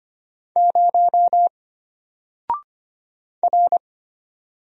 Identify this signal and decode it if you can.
Morse code, heard as a keyed tone, about 25 wpm, 700 Hz. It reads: 0 R
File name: Single-Letter-Number-25wpm